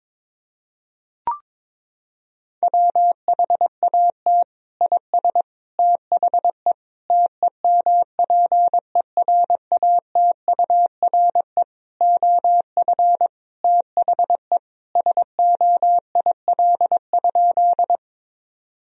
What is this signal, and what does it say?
Morse code, heard as a keyed tone, about 22 wpm, 700 Hz. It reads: WHAT IS THE TEMPERATURE OF THE SOIL?
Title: Sentences from Top 1000 Words - Review Part 4 - 22wpm